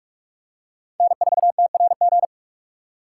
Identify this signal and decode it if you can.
Morse code, heard as a keyed tone, about 44 wpm, 700 Hz. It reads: N4TRG